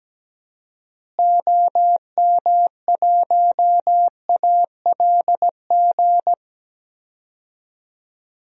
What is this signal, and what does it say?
Morse code, heard as a keyed tone, about 17 wpm, 700 Hz. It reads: OM1ALG
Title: DXCC-100 Call Signs - 6 Characters - Repeated 2x - 17wpm